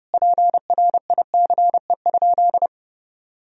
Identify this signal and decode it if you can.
Morse code, heard as a keyed tone, about 32 wpm, 700 Hz. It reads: PRICE?
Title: Sentences from Top 1000 Words - Review Part 4 - 30wpm